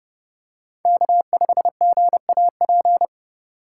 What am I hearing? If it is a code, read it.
Morse code, heard as a keyed tone, about 30 wpm, 700 Hz. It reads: K5GAP
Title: USA Call Signs - 2x Character Spacing - 30wpm